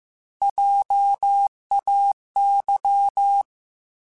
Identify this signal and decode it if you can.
Morse code, heard as a keyed tone, about 15 wpm, 800 Hz. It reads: JAY
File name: Alphabetical / Alphabetical mp3s / Js @13